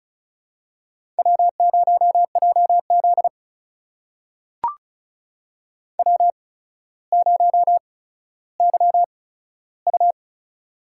Morse code, heard as a keyed tone, about 36 wpm, spaced as for 15 wpm, 700 Hz. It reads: W0JZ W0YU
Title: USA Call Signs - 4 Characters - 8x Character Spacing - 35wpm